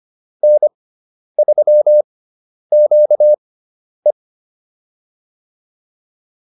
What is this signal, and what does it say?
Morse code, heard as a keyed tone, about 25 wpm, spaced as for 10 wpm, 600 Hz. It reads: N3QE